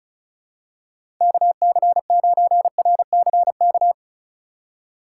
Morse code, heard as a keyed tone, about 36 wpm, 700 Hz. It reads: KC9RCK
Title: USA Call Signs - 6 Characters - 4x Character Spacing - 35wpm